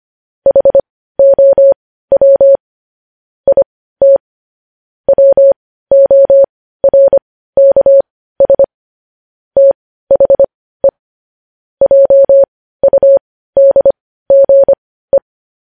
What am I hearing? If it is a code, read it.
Morse code, heard as a keyed tone, about 25 wpm, spaced as for 15 wpm, 550 Hz. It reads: HOW IT WORKS THE JUDGE
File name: podcast / News headlines-1x-25x15